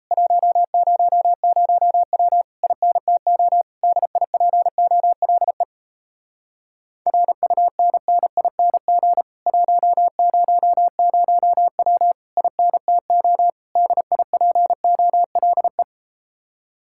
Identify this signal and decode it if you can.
Morse code, heard as a keyed tone, about 38 wpm, 700 Hz. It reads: 100W INTO DIPOLE RUNNING 100W INTO DIPOLE